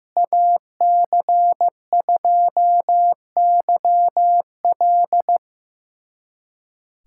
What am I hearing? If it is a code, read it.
Morse code, heard as a keyed tone, about 15 wpm, 700 Hz. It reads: AC2YL
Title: USA Callsign DE Callsign 15wpm